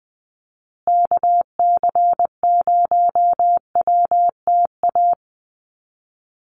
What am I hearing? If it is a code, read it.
Morse code, heard as a keyed tone, about 20 wpm, 700 Hz. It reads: KC0WTA
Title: USA Call Signs Repeated 3x 20wpm